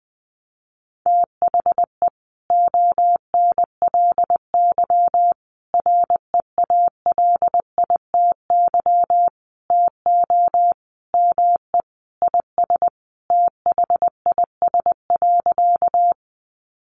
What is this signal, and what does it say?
Morse code, heard as a keyed tone, about 20 wpm, 700 Hz. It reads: THE ONLY REALITY TO ME IS THIS.
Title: Sentences from Top 1200 Words 20wpm